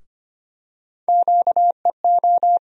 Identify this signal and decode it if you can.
Morse code, heard as a keyed tone, about 25 wpm, 700 Hz. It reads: QEO
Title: USA Call Sign Postfixes - 3 Letters 25wpm